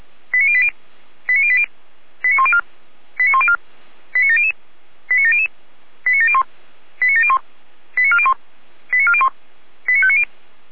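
Five-tone selective call (ZVEI), heard as quick runs of five tones, tones of about 70 ms. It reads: 89080 89080 89104 89104 89700 89700 89701 89701 89401 89401 89590
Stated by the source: ZVEI five-tone call received on SDR, from the feuerwehr-orlamuende.org server.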